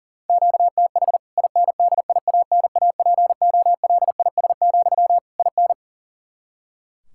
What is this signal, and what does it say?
Morse code, heard as a keyed tone, about 40 wpm, 700 Hz. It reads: QTH INDIANAPOLIS, IN
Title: K1USN SST QSO Practice 40wpm